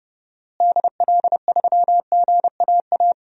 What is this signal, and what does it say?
Morse code, heard as a keyed tone, about 30 wpm, 700 Hz. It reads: DL3GAA
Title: DXCC-100 Call Signs - 6 Characters - Repeated 3x - 30wpm